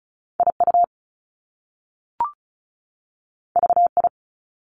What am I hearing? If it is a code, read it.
Morse code, heard as a keyed tone, about 36 wpm, 700 Hz. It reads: IU VI
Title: Top Two Letter Combinations in Words 35wpm